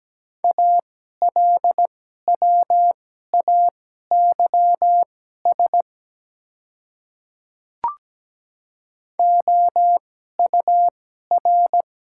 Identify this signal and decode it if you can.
Morse code, heard as a keyed tone, about 17 wpm, spaced as for 12 wpm, 700 Hz. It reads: ALWAYS OUR
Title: 2x Character Spacing - Top 300 Words - 17wpm